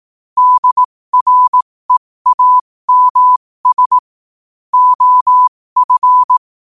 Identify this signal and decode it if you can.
Morse code, heard as a keyed tone, about 18 wpm, spaced as for 15 wpm, 1000 Hz. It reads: DREAMS OF
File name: qotdfiles / 15wpm